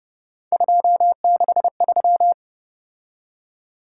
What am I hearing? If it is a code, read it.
Morse code, heard as a keyed tone, about 30 wpm, 700 Hz. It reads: 263